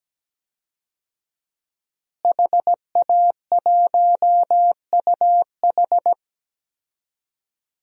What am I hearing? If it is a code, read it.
Morse code, heard as a keyed tone, about 17 wpm, 700 Hz. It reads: HA1UH
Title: DXCC-100 Call Signs - 5 Characters - Repeated 3x - 17wpm